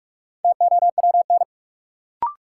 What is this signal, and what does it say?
Morse code, heard as a keyed tone, about 46 wpm, 700 Hz. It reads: TOWN
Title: Sentences from Top 500 Words - New Word Drill - 45wpm